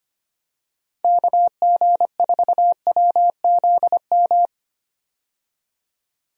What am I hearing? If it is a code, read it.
Morse code, heard as a keyed tone, about 25 wpm, 700 Hz. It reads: KG4WZM
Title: USA Call Signs - 6 Characters - 25wpm